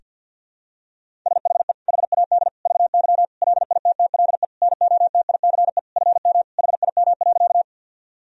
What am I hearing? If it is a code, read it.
Morse code, heard as a keyed tone, about 50 wpm, 700 Hz. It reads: SHE HAD VY LITTLE NOTICE FM HIM.
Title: Sentences from Top 400 Words and Abbreviations 50wpm